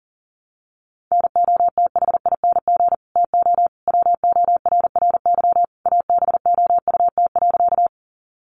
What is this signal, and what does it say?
Morse code, heard as a keyed tone, about 40 wpm, 700 Hz. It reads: NOTHING TO WORRY ABOUT.
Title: Sentences from Top 1100 Words 40wpm